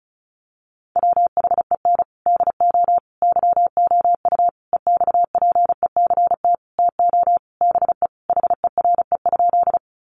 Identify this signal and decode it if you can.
Morse code, heard as a keyed tone, about 34 wpm, 700 Hz. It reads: WHEN DO YOU EXPECT TO BE HERE?
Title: Sentences from Top 1100 Words 35wpm